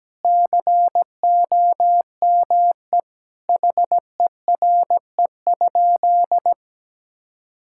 Morse code, heard as a keyed tone, about 17 wpm, 700 Hz. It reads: COME HERE?
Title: Sentences from Top 1000 Words - Review Part 5 - 17wpm